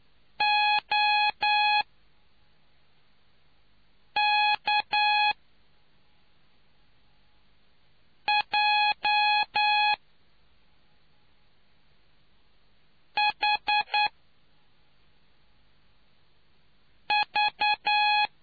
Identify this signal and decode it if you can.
Morse code, heard as a keyed tone, about 9 wpm, spaced as for 3 wpm, 840 Hz. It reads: OK J H V